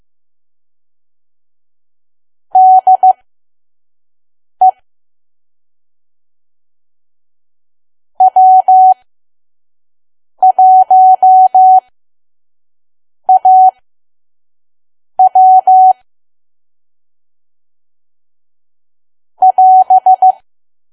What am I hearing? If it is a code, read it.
Morse code, heard as a keyed tone, about 15 wpm, spaced as for 5 wpm, 750 Hz. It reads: DE W1AW <AS>
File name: Archive / 5 WPM files / PM